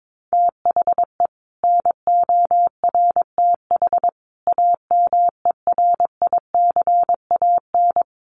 Morse code, heard as a keyed tone, about 22 wpm, 700 Hz. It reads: THE NORTH AMERICAN
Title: Sets of 5 Words Encore 22wpm